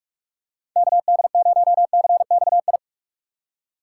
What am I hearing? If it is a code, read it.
Morse code, heard as a keyed tone, about 44 wpm, 700 Hz. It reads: KD0CXI
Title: USA Call Signs - 6 Characters - 2x Character Spacing - 45wpm